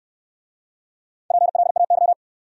Morse code, heard as a keyed tone, about 65 wpm, 690 Hz. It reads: PLAY